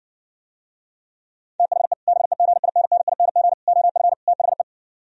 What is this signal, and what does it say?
Morse code, heard as a keyed tone, about 67 wpm, 700 Hz. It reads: THE BEGINNING OF THE